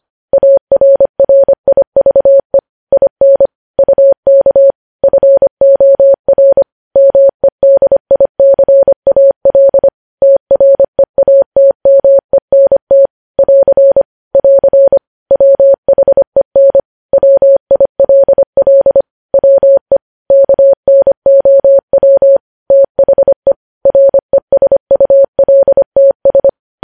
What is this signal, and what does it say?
Morse code, heard as a keyed tone, about 25 wpm, 550 Hz. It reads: ARRIVE IN UK FOR MEDICAL TREATMENT <AR> <AR> WHEN WILL WE KNOW THE RESULTS
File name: podcast / News headlines-1x-25x25